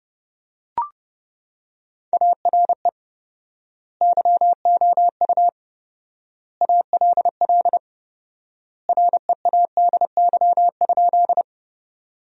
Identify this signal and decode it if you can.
Morse code, heard as a keyed tone, about 32 wpm, 700 Hz. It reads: ARE YOU ALL READY?
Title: 4x Word-Spacing - Sentences from Top 400 Words - 30wpm